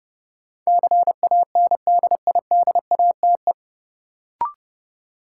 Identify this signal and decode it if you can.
Morse code, heard as a keyed tone, about 30 wpm, 700 Hz. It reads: CANDIDATE
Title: Sentences from Top 1600 Words - New Word Drill - 30wpm